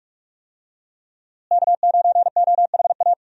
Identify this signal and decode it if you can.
Morse code, heard as a keyed tone, about 46 wpm, 700 Hz. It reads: K9OHA
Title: USA Call Signs - 5 Characters - Repeated 3x - 45wpm